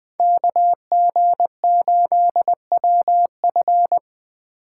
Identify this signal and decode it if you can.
Morse code, heard as a keyed tone, about 23 wpm, 700 Hz. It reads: KG8WF
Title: Speed Racing - USA 2x2 Call Signs 20wpm